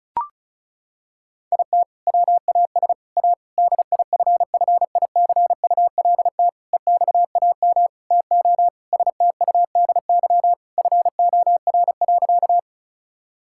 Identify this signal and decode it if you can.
Morse code, heard as a keyed tone, about 36 wpm, 690 Hz. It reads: IT WAS A DIFFICULT EXAM TO STUDY FOR.
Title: Sentences from Top 1000 Words - Review Part 5 - 35wpm